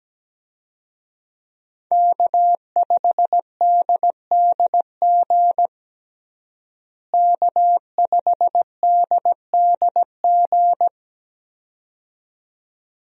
Morse code, heard as a keyed tone, about 17 wpm, 700 Hz. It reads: K5DDG K5DDG
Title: USA Call Signs - 5 Characters - Repeated 2x - 17wpm